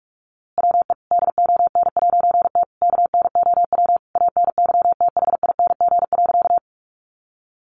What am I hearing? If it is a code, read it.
Morse code, heard as a keyed tone, about 44 wpm, 700 Hz. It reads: WE DON'T KNOW ANYTHING.
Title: A Sound Of Thunder 45wpm